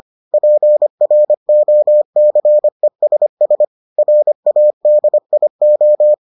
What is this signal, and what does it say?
Morse code, heard as a keyed tone, about 25 wpm, 600 Hz. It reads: PROCESS RADIO